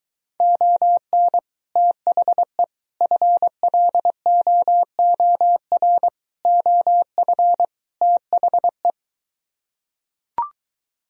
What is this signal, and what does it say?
Morse code, heard as a keyed tone, about 23 wpm, 700 Hz. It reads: ON THE FLOOR OF THE